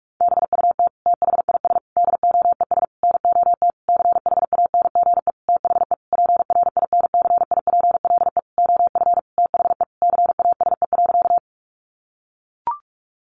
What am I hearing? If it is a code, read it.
Morse code, heard as a keyed tone, about 44 wpm, 700 Hz. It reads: BUT THIS DOES NOT CHANGE THE PRINCIPLE OF THE CASE.
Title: Sentences from Top 1400 Words 45wpm